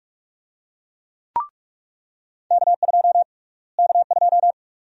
Morse code, heard as a keyed tone, about 45 wpm, 700 Hz. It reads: K2 K2
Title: USA Call Sign Prefixes - Repeated 2x - 45wpm